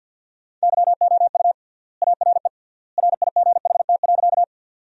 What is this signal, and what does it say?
Morse code, heard as a keyed tone, about 53 wpm, 700 Hz. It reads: YOU ARE RIGHT.
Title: 3x Word-Spacing - Sentences from Top 200 Words - 50wpm